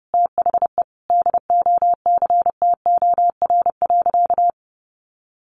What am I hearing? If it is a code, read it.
Morse code, heard as a keyed tone, about 30 wpm, 700 Hz. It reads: THE DOCTOR.
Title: Sentences from Top 1500 Words 30wpm